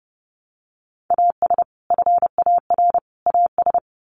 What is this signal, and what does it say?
Morse code, heard as a keyed tone, about 30 wpm, 700 Hz. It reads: AS FAR AS
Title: Sets of 3 Words 30wpm